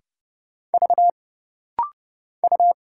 Morse code, heard as a keyed tone, about 30 wpm, 700 Hz. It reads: V U
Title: Single Letter-Number Alt - Rapid-Fire 30wpm